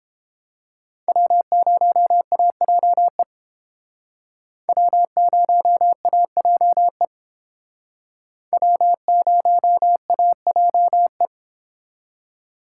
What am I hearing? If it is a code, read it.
Morse code, heard as a keyed tone, about 30 wpm, 700 Hz. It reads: W0AJE W0AJE W0AJE